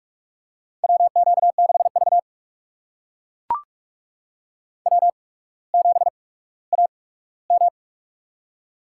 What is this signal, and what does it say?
Morse code, heard as a keyed tone, about 44 wpm, spaced as for 19 wpm, 690 Hz. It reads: WQ6V W7AM
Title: USA Call Signs - 4 Characters - 8x Character Spacing - 45wpm